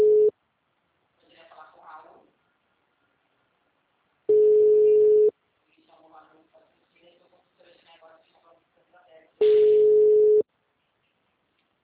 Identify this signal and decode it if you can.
Ringback tone (Europe, 425 Hz), with speech.
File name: out- -2051- - - .70338